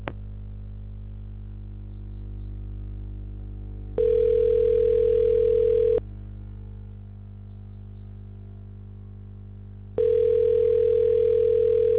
Ringback tone (North America), 440 + 480 Hz.